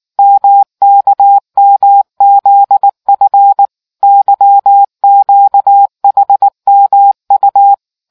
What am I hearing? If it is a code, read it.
Morse code, heard as a keyed tone, about 19 wpm, 800 Hz. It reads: MKMZF YQHMU